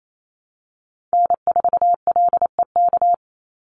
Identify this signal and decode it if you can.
Morse code, heard as a keyed tone, about 28 wpm, 700 Hz. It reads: N4LEK
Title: USA Call Signs 28wpm